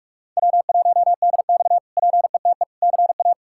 Morse code, heard as a keyed tone, about 43 wpm, 700 Hz. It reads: W1DX PETE CA
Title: K1USN SST QSO Practice - 45wpm